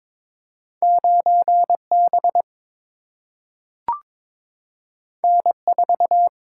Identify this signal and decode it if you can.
Morse code, heard as a keyed tone, about 23 wpm, 700 Hz. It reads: 9 B N 4
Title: Single-Letter Single-Number 22wpm